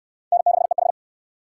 Morse code, heard as a keyed tone, about 68 wpm, 700 Hz. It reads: N6EH